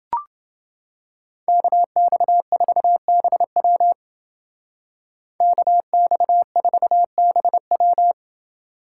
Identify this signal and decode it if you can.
Morse code, heard as a keyed tone, about 29 wpm, 700 Hz. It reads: KX4BW KX4BW